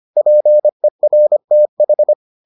Morse code, heard as a keyed tone, about 25 wpm, 600 Hz. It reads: PERTH